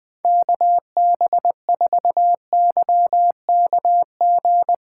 Morse code, heard as a keyed tone, about 20 wpm, 700 Hz. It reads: KB4YKG
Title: USA Callsign DE Callsign 20wpm